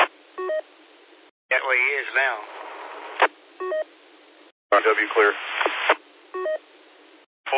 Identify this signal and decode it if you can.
Speech. There is no code.